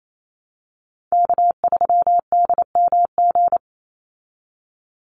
Morse code, heard as a keyed tone, about 28 wpm, 700 Hz. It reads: K3DMG
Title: Call Signs Repeated 2x - 28wpm